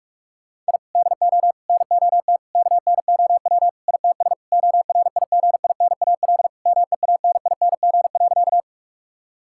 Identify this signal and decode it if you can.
Morse code, heard as a keyed tone, about 46 wpm, 700 Hz. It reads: I DO NOT KNOW ITS ORIGINAL MEANING.
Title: Sentences from Top 1000 Words - Review Part 6 - 45wpm